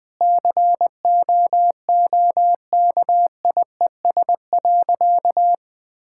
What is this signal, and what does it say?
Morse code, heard as a keyed tone, about 20 wpm, 700 Hz. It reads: COOKIES.
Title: Sentences with 5-Word Phrases - Spoken Beginning - 20wpm